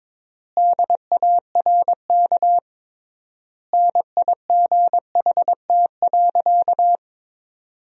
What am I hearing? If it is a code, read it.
Morse code, heard as a keyed tone, about 22 wpm, 700 Hz. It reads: DARK NIGHT.
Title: 3x Word-Spacing - Sentences from Top 500 Words - 22wpm